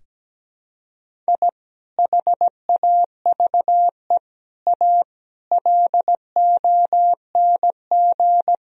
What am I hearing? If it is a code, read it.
Morse code, heard as a keyed tone, about 17 wpm, 700 Hz. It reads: I HAVE A LONG
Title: Sentences from Top 1000 Words - Review Part 1 - 17wpm